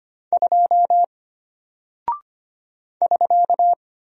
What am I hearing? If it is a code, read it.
Morse code, heard as a keyed tone, about 25 wpm, 700 Hz. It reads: 2 <SK>
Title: Single Character (Letter:Number:Punctuation:Prosign) Alt 25wpm